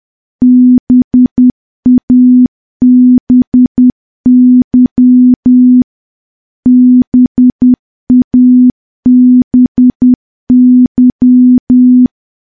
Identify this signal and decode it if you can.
Morse code, heard as a keyed tone, about 10 wpm, 250 Hz. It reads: BABY BABY
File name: morse / morse-training-10WPM